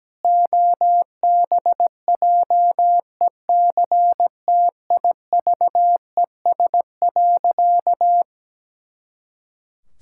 Morse code, heard as a keyed tone, about 17 wpm, 700 Hz. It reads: OBJECTIVES.
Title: Pangram Challenge v2 - 17wpm